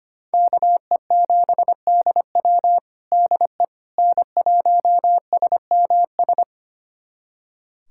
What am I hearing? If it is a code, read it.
Morse code, heard as a keyed tone, about 25 wpm, 700 Hz. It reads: KE7DW DE N1SMS
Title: USA Callsign DE Callsign - 5 Characters - 25wpm